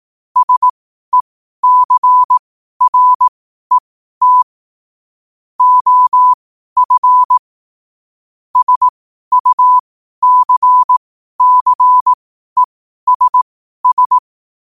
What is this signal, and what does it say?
Morse code, heard as a keyed tone, about 18 wpm, spaced as for 12 wpm, 1000 Hz. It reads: SECRET OF SUCCESS
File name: qotdfiles / 12wpm